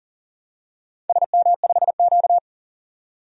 Morse code, heard as a keyed tone, about 38 wpm, 690 Hz. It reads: SM5Q
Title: DXCC-100 Call Signs - 4 Characters - 40wpm